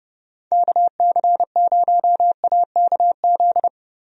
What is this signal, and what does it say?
Morse code, heard as a keyed tone, about 31 wpm, 700 Hz. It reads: KC0AKZ
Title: USA Call Signs - 6 Characters - Repeated 2x - 30wpm